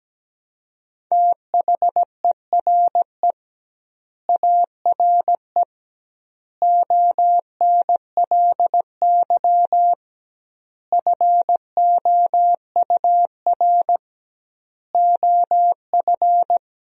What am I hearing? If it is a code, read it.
Morse code, heard as a keyed tone, about 17 wpm, 700 Hz. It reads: THERE ARE ONLY FOUR OF
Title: 2x Word-Spacing - Sentences from Top 300 Words - 17wpm